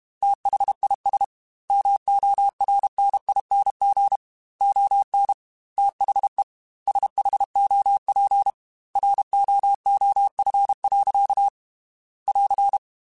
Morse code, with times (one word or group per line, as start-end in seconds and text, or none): 0.2-1.2 THIS
1.7-4.2 MORNING
4.6-5.3 ON
5.8-6.4 THE
6.9-8.5 SHOP
9.0-11.5 ROOF.
12.3-12.8 <AR>